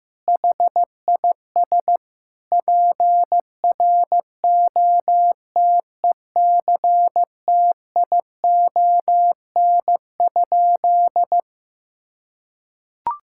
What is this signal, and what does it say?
Morse code, heard as a keyed tone, about 15 wpm, 700 Hz. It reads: HIS PROTECTION?
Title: Sentences from Top 1600 Words 15wpm